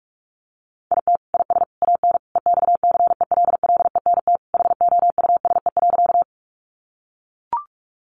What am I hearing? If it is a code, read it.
Morse code, heard as a keyed tone, about 44 wpm, 700 Hz. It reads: IT IS AN EXCELLENT HOUSE.
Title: Sentences from Top 1000 Words 45wpm